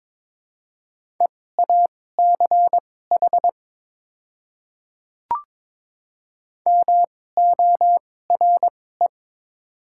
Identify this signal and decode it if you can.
Morse code, heard as a keyed tone, about 22 wpm, spaced as for 16 wpm, 700 Hz. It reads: EACH MORE
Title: 2x Character Spacing - Top 200 Words - 22wpm